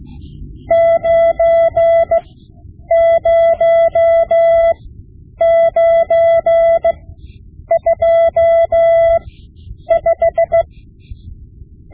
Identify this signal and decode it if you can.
Morse code, heard as a keyed tone, about 16 wpm, spaced as for 9 wpm, 670 Hz. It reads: 90925
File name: Morse Lessons / Morse Lessons / Tape-5B